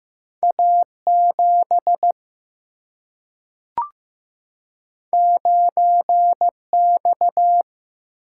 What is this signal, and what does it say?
Morse code, heard as a keyed tone, about 15 wpm, 700 Hz. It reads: A 7 9 X
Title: Single-Letter Single-Number 15wpm